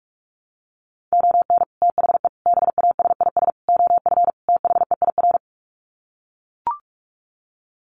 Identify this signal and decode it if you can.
Morse code, heard as a keyed tone, about 44 wpm, 700 Hz. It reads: ON THE BASIS OF THEIR
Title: Sets of 5 Words Encore 45wpm